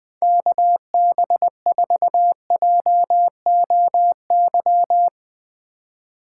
Morse code, heard as a keyed tone, about 20 wpm, 700 Hz. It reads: KB4JOY
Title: 2023 POTA Activator Call Signs - 20wpm